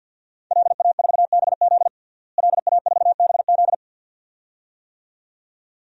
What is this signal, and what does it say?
Morse code, heard as a keyed tone, about 51 wpm, 700 Hz. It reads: LA4BZ LA4BZ
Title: DXCC-100 Call Signs - 5 Characters - Repeated 2x - 50wpm